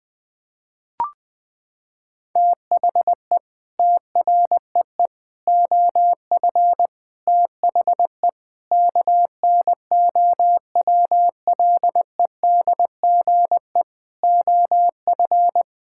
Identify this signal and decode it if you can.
Morse code, heard as a keyed tone, about 20 wpm, 700 Hz. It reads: THE TREE OF THE KNOWLEDGE OF
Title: Sets of 9 Words - 20wpm